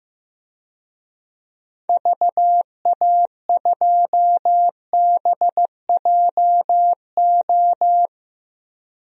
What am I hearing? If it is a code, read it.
Morse code, heard as a keyed tone, about 15 wpm, 700 Hz. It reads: VA2BJO
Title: DXCC-100 Call Signs - 6 Characters - 15wpm